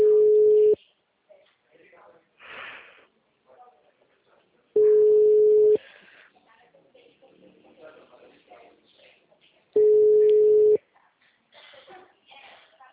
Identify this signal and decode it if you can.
Ringback tone (Europe, 425 Hz), with speech.